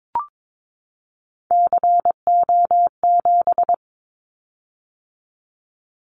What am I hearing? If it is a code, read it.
Morse code, heard as a keyed tone, about 22 wpm, 700 Hz. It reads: CO7
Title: Callsign Prefixes for Top 100 DXCC - 22wpm